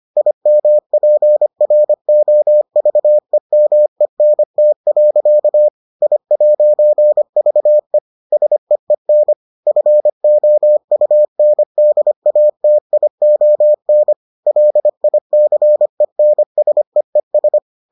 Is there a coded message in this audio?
Yes: Morse code from 0.2 to 17.6 s, heard as a keyed tone, about 25 wpm, 600 Hz. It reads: IMPROVEMENT. I'VE SEEN FOUNDATION LICENSEES